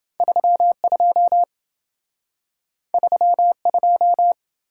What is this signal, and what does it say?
Morse code, heard as a keyed tone, about 26 wpm, 700 Hz. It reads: 32 32